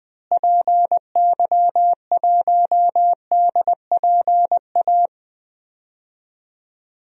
Morse code, heard as a keyed tone, about 20 wpm, 700 Hz. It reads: PY1DPA
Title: DXCC-100 Call Signs - 6 Characters - Repeated 2x - 20wpm